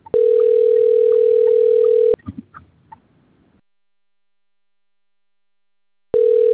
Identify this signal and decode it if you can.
Ringback tone (North America), 440 + 480 Hz.